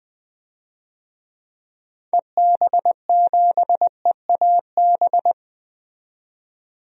Morse code, heard as a keyed tone, about 20 wpm, 700 Hz. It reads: EB7EAB
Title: DXCC-100 Call Signs - 6 Characters - 20wpm